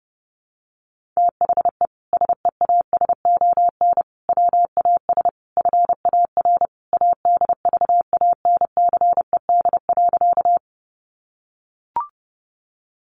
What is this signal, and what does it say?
Morse code, heard as a keyed tone, about 30 wpm, 700 Hz. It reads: THE SEASON WAS FAR ADVANCED.
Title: Sentences from Top 1400 Words 30wpm